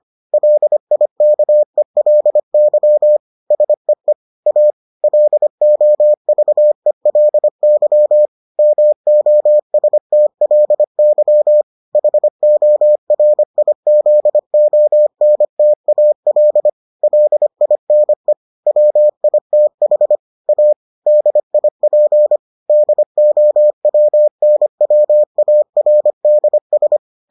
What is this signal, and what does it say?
Morse code, heard as a keyed tone, about 25 wpm, 600 Hz. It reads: LIKELY SEE A LOVELY MOSTLY HORIZONTAL LINE WITH A DIP DOWNWARDS